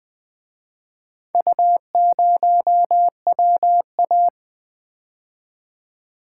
Morse code, heard as a keyed tone, about 24 wpm, 700 Hz. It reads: U0WA
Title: Speed-Racing - Four Letter-Number Combinations - 20wpm